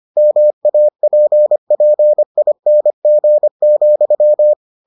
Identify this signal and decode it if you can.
Morse code, heard as a keyed tone, about 25 wpm, 600 Hz. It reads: MAPPING,